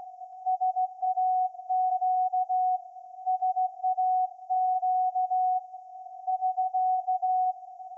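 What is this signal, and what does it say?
Morse code, heard as a keyed tone, about 15 wpm, 740 Hz. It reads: SAQ SAQ <SK>
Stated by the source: Morse code from SAQ, received at a station in Putignano, Apulia, Italy, on VLF.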